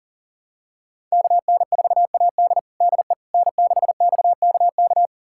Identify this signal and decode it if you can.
Morse code, heard as a keyed tone, about 40 wpm, 700 Hz. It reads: KN4AD DE N6XKK
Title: USA Callsign DE Callsign - 5 Characters - 40wpm